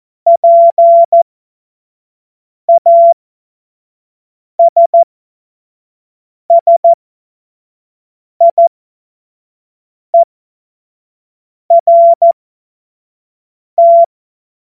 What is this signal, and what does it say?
Morse code, heard as a keyed tone, about 14 wpm, spaced as for 5 wpm, 680 Hz. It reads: PASSIERT
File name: Morsekurs / Witze / Witz 02. August 2022 25 F70